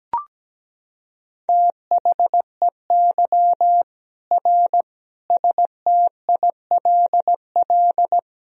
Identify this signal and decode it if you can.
Morse code, heard as a keyed tone, about 17 wpm, 700 Hz. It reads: THEY R STILL
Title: Sentences from Top 400 Words and Abbreviations 17wpm